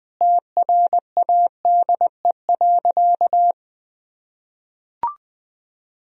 Morse code, heard as a keyed tone, about 20 wpm, 700 Hz. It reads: TRADE.
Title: Sentences from Top 1000 Words - Review Part 4 - 20wpm